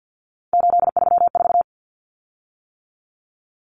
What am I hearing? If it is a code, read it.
Morse code, heard as a keyed tone, about 50 wpm, 700 Hz. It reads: 834
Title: Common Cut Numbers - 3-Digit 50wpm